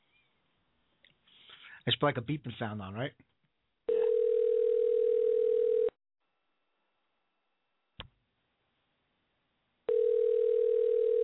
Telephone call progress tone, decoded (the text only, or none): ringback tone